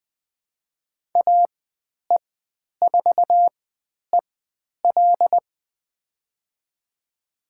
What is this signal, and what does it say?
Morse code, heard as a keyed tone, about 20 wpm, spaced as for 10 wpm, 700 Hz. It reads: AE4EL
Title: 095 Lesson 16g - Letter c - Call Signs 20wpm